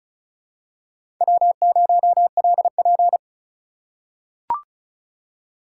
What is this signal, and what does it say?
Morse code, heard as a keyed tone, about 34 wpm, 700 Hz. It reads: W0LP
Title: USA Call Signs - 4 Characters - 2x Character Spacing - 35wpm